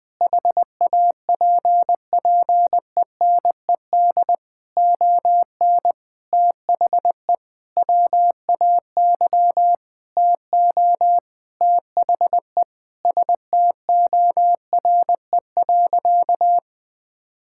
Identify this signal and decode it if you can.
Morse code, heard as a keyed tone, about 20 wpm, 700 Hz. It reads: HAPPENED ON THE WAY TO THE STORE.
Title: Sentences with 8-Word Phrases - Spoken Beginning - 20wpm